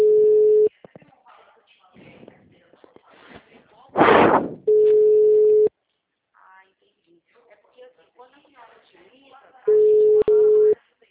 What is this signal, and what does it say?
Ringback tone (Europe, 425 Hz), with speech.